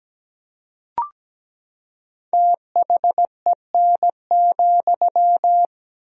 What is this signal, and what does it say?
Morse code, heard as a keyed tone, about 17 wpm, 700 Hz. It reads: THEN,
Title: 3x Word-Spacing - Sentences from Top 300 Words - 17wpm